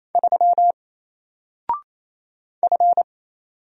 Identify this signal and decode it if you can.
Morse code, heard as a keyed tone, about 28 wpm, 700 Hz. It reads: 3 F
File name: Single-Letter-Number-28wpm